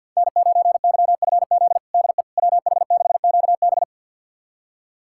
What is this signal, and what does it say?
Morse code, heard as a keyed tone, about 46 wpm, 700 Hz. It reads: N9YFZ DE WH6QB
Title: USA Callsign DE Callsign - 5 Characters - 50wpm